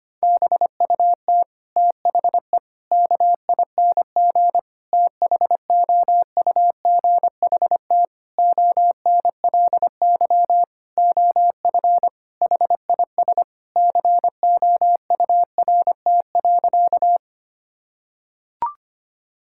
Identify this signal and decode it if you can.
Morse code, heard as a keyed tone, about 25 wpm, 700 Hz. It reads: BUT THE KING THOUGHT ONLY OF HIS COURT.
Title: Sentences from Top 1600 Words 25wpm